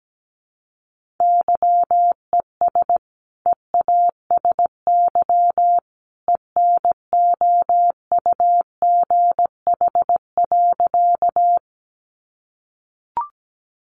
Morse code, heard as a keyed tone, about 17 wpm, 700 Hz. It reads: YES EASY ENOUGH.
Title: Sentences from Top 700 Words 17wpm